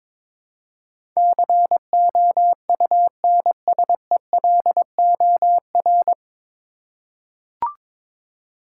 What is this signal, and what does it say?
Morse code, heard as a keyed tone, about 21 wpm, 700 Hz. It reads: COUNSELOR
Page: Morse Code Ninja practice 22wpm